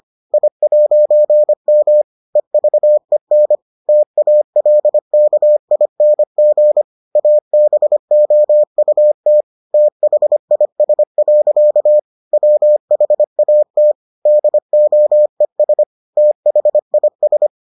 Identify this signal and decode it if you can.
Morse code, heard as a keyed tone, about 25 wpm, 600 Hz. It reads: I'M EVEN TALKING ABOUT THIS. WHAT DOES THIS